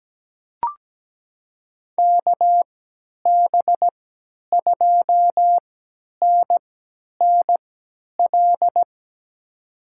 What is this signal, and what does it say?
Morse code, heard as a keyed tone, about 17 wpm, spaced as for 10 wpm, 700 Hz. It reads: KB2NNL
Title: USA Call Signs - 6 Characters - 3x Character Spacing - 17wpm